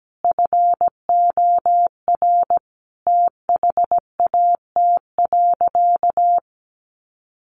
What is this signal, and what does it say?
Morse code, heard as a keyed tone, about 17 wpm, 700 Hz. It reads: FOR THAT.
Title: Sentences from Top 1100 Words 17wpm